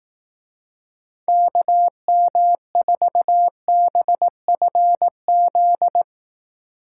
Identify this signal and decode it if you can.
Morse code, heard as a keyed tone, about 18 wpm, 700 Hz. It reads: KM4BFZ
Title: Speed Racing - USA Call Signs 15wpm